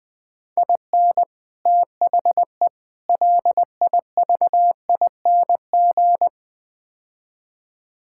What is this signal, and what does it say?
Morse code, heard as a keyed tone, about 20 wpm, 700 Hz. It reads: IN THE LIVING
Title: Build A Phrase - 4-Word Phrases - 20wpm